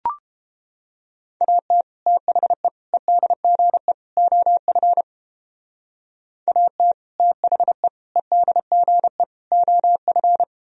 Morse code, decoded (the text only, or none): AT THE EDGE OF AT THE EDGE OF